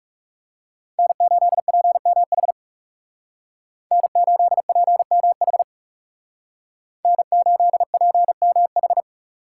Morse code, heard as a keyed tone, about 37 wpm, 700 Hz. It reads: N8PMH N8PMH N8PMH
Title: Speed Racing - USA 1x3 Call Signs 30wpm